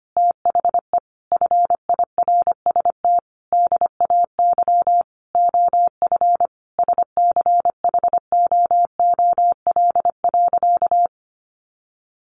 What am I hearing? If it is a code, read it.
Morse code, heard as a keyed tone, about 25 wpm, 700 Hz. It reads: THE FIRST DAY OF SCHOOL.
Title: Additional Easy Phrases 25wpm